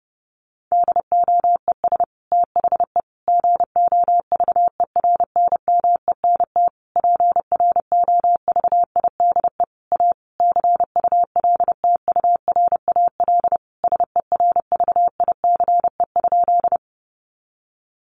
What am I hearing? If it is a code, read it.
Morse code, heard as a keyed tone, about 30 wpm, 700 Hz. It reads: DOES THE GOVERNMENT PROVIDE A CULTURAL SERVICE?
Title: Sentences from Top 1500 Words 30wpm